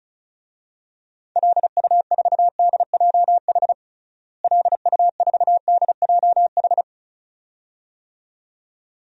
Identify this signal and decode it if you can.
Morse code, heard as a keyed tone, about 36 wpm, 700 Hz. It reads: LU4DJH LU4DJH
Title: DXCC-100 Call Signs - 6 Characters - Repeated 2x - 35wpm